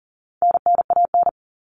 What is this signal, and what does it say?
Morse code, heard as a keyed tone, about 40 wpm, 700 Hz. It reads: NNAN
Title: All Cut Numbers - 4-Digit 40wpm